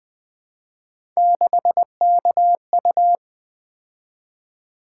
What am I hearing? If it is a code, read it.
Morse code, heard as a keyed tone, about 20 wpm, 700 Hz. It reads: 6KU
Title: Three Letter-Number Combinations - 20wpm